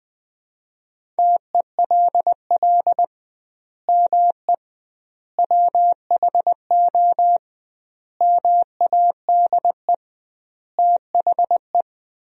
Morse code, heard as a keyed tone, about 20 wpm, 700 Hz. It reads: TELL ME WHO MADE THE
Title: 2x Word-Spacing - Sentences from Top 200 Words - 20wpm